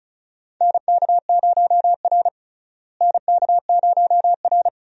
Morse code, heard as a keyed tone, about 36 wpm, 700 Hz. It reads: NK0R NK0R
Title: USA Call Signs - 4 Characters - Repeated 2x - 35wpm